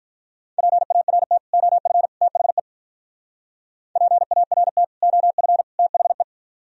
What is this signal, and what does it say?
Morse code, heard as a keyed tone, about 44 wpm, 700 Hz. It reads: PART OF THE PART OF THE